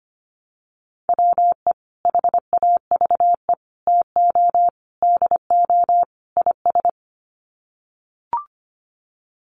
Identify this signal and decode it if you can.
Morse code, heard as a keyed tone, about 25 wpm, 700 Hz. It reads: WE HAVE TO DO IS
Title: Sets of 5 Words Encore 25wpm